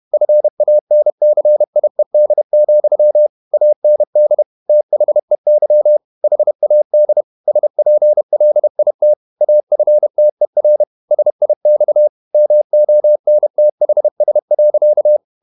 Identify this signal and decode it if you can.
Morse code, heard as a keyed tone, about 31 wpm, 600 Hz. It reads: FANCIED, AND THEY HAD SPLIT AFTER SIX MONTHS.
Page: Never 31wpm | CW med Gnister